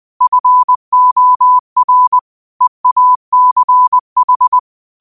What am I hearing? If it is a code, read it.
Morse code, heard as a keyed tone, about 20 wpm, 1000 Hz. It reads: FOR EACH
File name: qotdfiles / 20wpm